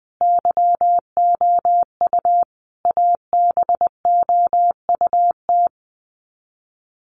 Morse code, heard as a keyed tone, about 20 wpm, 700 Hz. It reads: YOU ABOUT
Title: Sets of 5 Words Encore2 20wpm